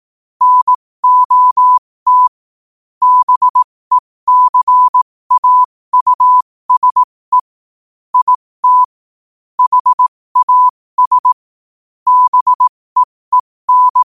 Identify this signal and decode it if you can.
Morse code, heard as a keyed tone, about 18 wpm, spaced as for 15 wpm, 1000 Hz. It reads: NOT BECAUSE IT HAS BEEN
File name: qotdfiles / 15wpm